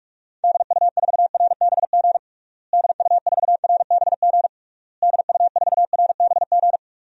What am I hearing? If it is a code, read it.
Morse code, heard as a keyed tone, about 42 wpm, 700 Hz. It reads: DU4RBG DU4RBG DU4RBG
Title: DXCC-100 Call Signs - 6 Characters - Repeated 3x - 45wpm